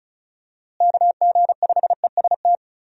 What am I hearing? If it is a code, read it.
Morse code, heard as a keyed tone, about 36 wpm, 700 Hz. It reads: KG5EST
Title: USA Call Signs - 6 Characters - Repeated 3x - 35wpm